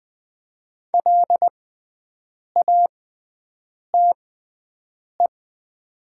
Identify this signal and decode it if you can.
Morse code, heard as a keyed tone, about 20 wpm, spaced as for 7 wpm, 700 Hz. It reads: LATE
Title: 6x Character Spacing - Top 500 Words - 20wpm